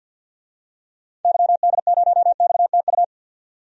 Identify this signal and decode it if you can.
Morse code, heard as a keyed tone, about 52 wpm, 700 Hz. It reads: YD0XTU
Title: DXCC-100 Call Signs - 6 Characters - 50wpm